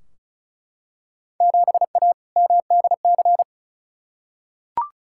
Morse code, heard as a keyed tone, about 33 wpm, 690 Hz. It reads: 7A MDC
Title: Field Day Exchanges Encore - 35wpm